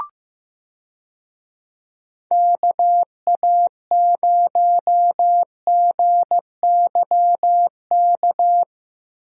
Morse code, heard as a keyed tone, about 15 wpm, 700 Hz. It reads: KA0GYK
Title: USA Call Signs - 6 Characters - Repeated 2x - 15wpm